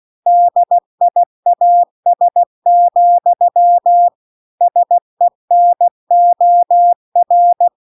Morse code, heard as a keyed tone, about 16 wpm, 700 Hz. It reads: DIAS, SENOR